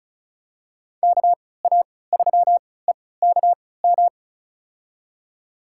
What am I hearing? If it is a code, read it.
Morse code, heard as a keyed tone, about 36 wpm, spaced as for 20 wpm, 700 Hz. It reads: KA3EKM
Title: USA Call Signs - 6 Characters - 3x Character Spacing - 35wpm